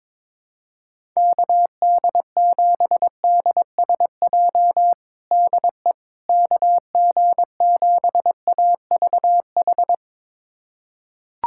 Morse code, heard as a keyed tone, about 22 wpm, 700 Hz. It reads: KD7DSJ DE KG7AVH
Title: USA Callsign DE Callsign 22wpm